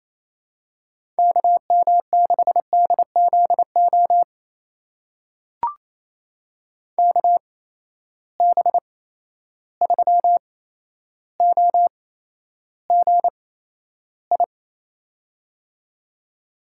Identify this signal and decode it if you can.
Morse code, heard as a keyed tone, about 28 wpm, spaced as for 12 wpm, 700 Hz. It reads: KM6DZO KB3OGI
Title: USA Call Signs - 6 Characters - 8x Character Spacing - 28wpm